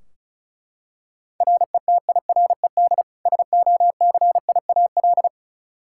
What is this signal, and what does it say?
Morse code, heard as a keyed tone, about 34 wpm, 700 Hz. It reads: RETIRED SOCIAL
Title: K1USN SST QSO Practice 35wpm